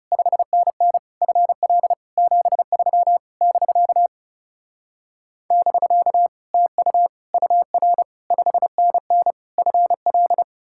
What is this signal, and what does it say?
Morse code, heard as a keyed tone, about 40 wpm, 700 Hz. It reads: 5NN FL 73 <BK> <BK> TU UR 5NN FL